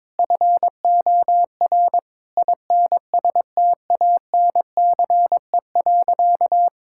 Morse code, heard as a keyed tone, about 22 wpm, 700 Hz. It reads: FOR INSTANCE.
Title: Sentences from Top 1000 Words - Review Part 5 - 22wpm